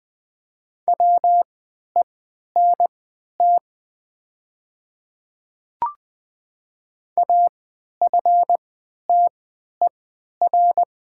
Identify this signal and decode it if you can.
Morse code, heard as a keyed tone, about 20 wpm, spaced as for 11 wpm, 700 Hz. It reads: WENT AFTER
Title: 3x Character Spacing - Top 300 Words - 20wpm